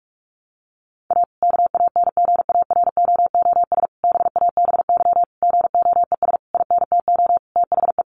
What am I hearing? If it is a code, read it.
Morse code, heard as a keyed tone, about 46 wpm, 700 Hz. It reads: A KANGAROOS BABY GOES INTO THE
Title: Easy Phrases 45wpm